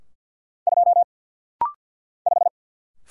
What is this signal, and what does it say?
Morse code, heard as a keyed tone, about 50 wpm, 700 Hz. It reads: 2 5
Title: Single Letter-Number - Rapid-Fire 50wpm